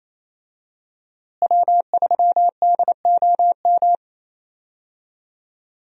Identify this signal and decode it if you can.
Morse code, heard as a keyed tone, about 28 wpm, 700 Hz. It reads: W3DOM
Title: USA Call Signs - 5 Characters - Repeated 2x - 28wpm